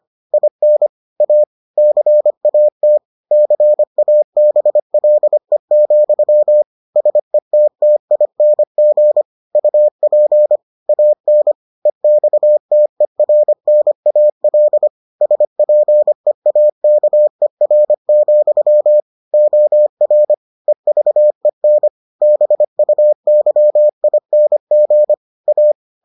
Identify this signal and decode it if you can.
Morse code, heard as a keyed tone, about 25 wpm, 600 Hz. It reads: IN A CAT CABLE, SETTING UP AN EXTERNAL SPEAKER, OR EVEN BUYING A